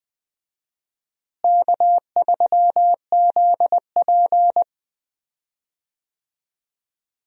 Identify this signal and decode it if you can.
Morse code, heard as a keyed tone, about 20 wpm, 700 Hz. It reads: K3ZP
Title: USA Call Signs - 4 Characters - Repeated 3x - 20wpm